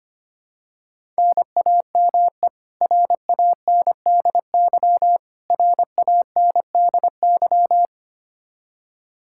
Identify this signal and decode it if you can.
Morse code, heard as a keyed tone, about 25 wpm, 700 Hz. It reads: NAME RANDY RANDY
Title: Practice QSO - Exchange 1 - Beginning - 25wpm